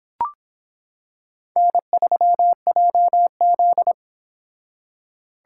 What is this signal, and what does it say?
Morse code, heard as a keyed tone, about 25 wpm, 700 Hz. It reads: N3JZ